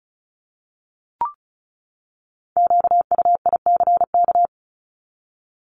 Morse code, heard as a keyed tone, about 34 wpm, 700 Hz. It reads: QUICK
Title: Top 500 Words 35wpm